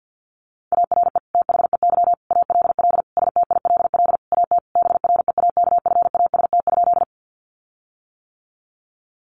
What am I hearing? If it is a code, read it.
Morse code, heard as a keyed tone, about 50 wpm, 700 Hz. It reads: ARE THEY ALL STILL AT BREAKFAST?
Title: Sentences from Top 1500 Words 50wpm